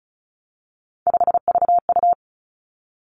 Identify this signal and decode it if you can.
Morse code, heard as a keyed tone, about 36 wpm, 700 Hz. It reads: 5VU
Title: Three Challenging Letter-Number Combinations 35wpm